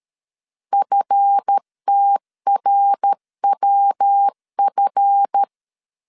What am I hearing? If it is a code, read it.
Morse code, heard as a keyed tone, about 13 wpm, 800 Hz. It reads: FTRWF
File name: LEZIONI / 7 GF9Z / 7 GF 12wpm 2122